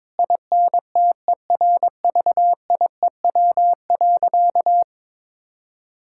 Morse code, heard as a keyed tone, about 22 wpm, 700 Hz. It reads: INTERVIEW.
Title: Sentences from Top 1600 Words 22wpm